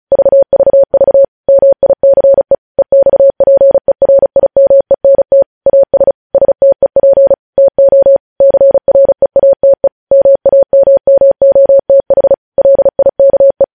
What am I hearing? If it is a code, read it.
Morse code, heard as a keyed tone, about 36 wpm, 550 Hz. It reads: VVV MICE EXPERIMENT AS STEP TO CREATE MAMMOTH LIKE
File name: podcast / News headlines-1x-35x35